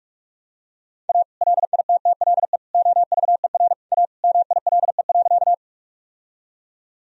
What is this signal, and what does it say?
Morse code, heard as a keyed tone, about 46 wpm, 700 Hz. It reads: A LITTLE OVER A MILE.
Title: Sentences from Top 1000 Words - Review Part 1 - 45wpm